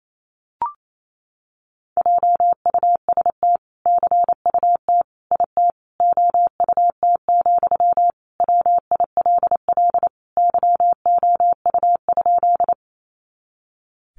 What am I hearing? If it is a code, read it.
Morse code, heard as a keyed tone, about 28 wpm, 700 Hz. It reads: JUST CUT IT OUT, WILL YOU?
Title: Sentences from Top 300 Words 28wpm